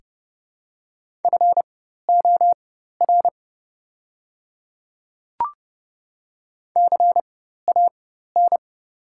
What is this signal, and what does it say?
Morse code, heard as a keyed tone, about 30 wpm, spaced as for 14 wpm, 700 Hz. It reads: FOR CAN
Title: 4x Character Spacing - Top 200 Words - 30wpm